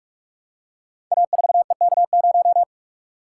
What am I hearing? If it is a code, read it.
Morse code, heard as a keyed tone, about 44 wpm, 700 Hz. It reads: A4EK0
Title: Code Groups - 5 Letter:Numbers - 45wpm